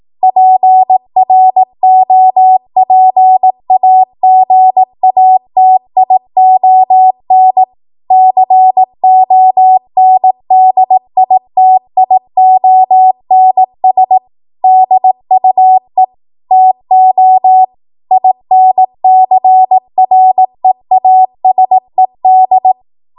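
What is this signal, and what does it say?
Morse code, heard as a keyed tone, about 18 wpm, 750 Hz. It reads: PROPAGATION CONDITIONS DUE TO INCREASED
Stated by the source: 18 WPM Morse code practice from a national amateur radio association's code archive.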